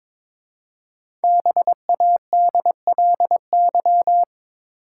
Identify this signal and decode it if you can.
Morse code, heard as a keyed tone, about 22 wpm, 700 Hz. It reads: BADLY
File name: 5 Letter Words - First 3 Letters Given - 22wpm